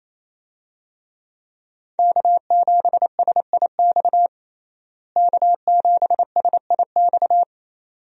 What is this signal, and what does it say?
Morse code, heard as a keyed tone, about 28 wpm, 700 Hz. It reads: K7SIX K7SIX
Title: USA Call Signs - 5 Characters - Repeated 2x - 28wpm